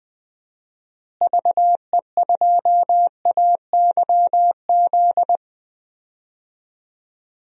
Morse code, heard as a keyed tone, about 20 wpm, 700 Hz. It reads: VE2AYZ
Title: DXCC-100 Call Signs - 6 Characters - Repeated 3x - 20wpm